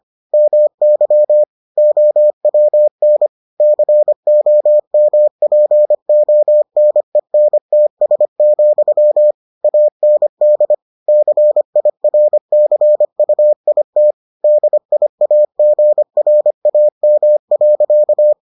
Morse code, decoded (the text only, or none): MY OWN COMPONENTS, AND CIRCUIT DIAGRAM.